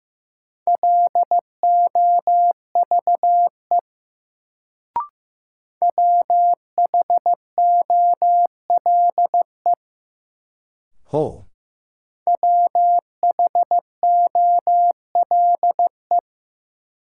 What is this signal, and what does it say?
Morse code, heard as a keyed tone, about 15 wpm, 700 Hz. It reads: LOVE WHOLE WHOLE
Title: Sentences from Top 400 Words - New Word Drill - 15wpm